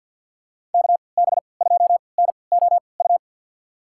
Morse code, heard as a keyed tone, about 47 wpm, spaced as for 27 wpm, 700 Hz. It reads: KB2NOU
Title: USA Call Signs - 6 Characters - 3x Character Spacing - 50wpm